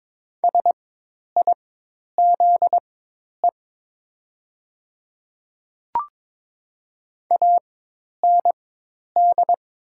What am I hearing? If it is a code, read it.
Morse code, heard as a keyed tone, about 22 wpm, spaced as for 10 wpm, 700 Hz. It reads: SIZE AND
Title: 4x Character Spacing - Top 500 Words - 22wpm